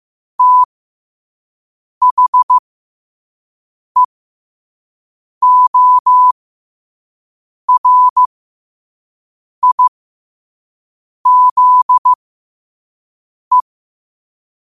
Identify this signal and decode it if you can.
Morse code, heard as a keyed tone, about 15 wpm, spaced as for 5 wpm, 1000 Hz. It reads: THEORIZE
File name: qotdfiles / 5wpm